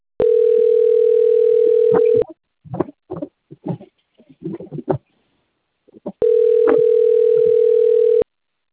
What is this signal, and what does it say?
Ringback tone (North America), 440 + 480 Hz.